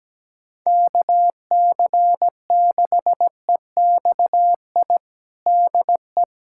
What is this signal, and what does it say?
Morse code, heard as a keyed tone, about 17 wpm, 700 Hz. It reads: KC6EXI DE
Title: USA Callsign DE Callsign 17wpm